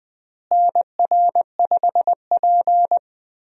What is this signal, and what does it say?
Morse code, heard as a keyed tone, about 20 wpm, 700 Hz. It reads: NR5P
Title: USA Call Signs - 4 Characters - Repeated 3x - 20wpm